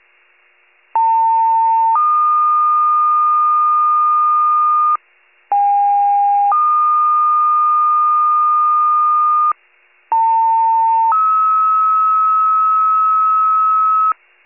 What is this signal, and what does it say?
Two-tone page, heard as a one-second tone, then a three-second tone, about 900 then 1220 Hz, about 810 then 1220 Hz, about 900 then 1290 Hz. It is heard from 0.9 to 14.1 s.
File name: wcalert